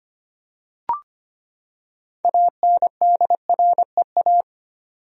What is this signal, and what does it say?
Morse code, heard as a keyed tone, about 25 wpm, 700 Hz. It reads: ANDREA
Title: USA Common Names - 25wpm